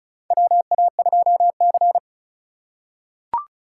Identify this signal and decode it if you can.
Morse code, heard as a keyed tone, about 36 wpm, 700 Hz. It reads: WA2C
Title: USA Call Signs - 4 Characters - 4x Character Spacing - 35wpm